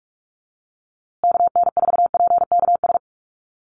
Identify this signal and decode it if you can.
Morse code, heard as a keyed tone, about 44 wpm, 700 Hz. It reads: KN4PKS
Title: USA Call Signs Repeated 2x 45wpm